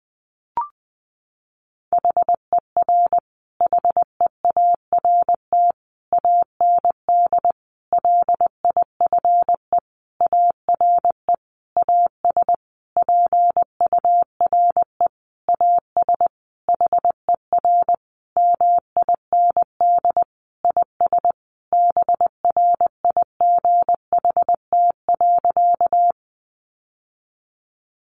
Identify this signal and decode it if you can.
Morse code, heard as a keyed tone, about 20 wpm, 700 Hz. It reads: HER HEART AND LIFE ARE AS PURE AS HER MIND IS BRIGHT.
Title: Sentences from Top 1500 Words 20wpm